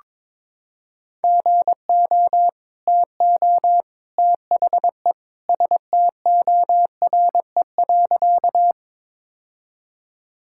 Morse code, heard as a keyed tone, about 21 wpm, 700 Hz. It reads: GO TO THE STORE.
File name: practice / 22wpm / Sentences with 2-Word Phrases - 22wpm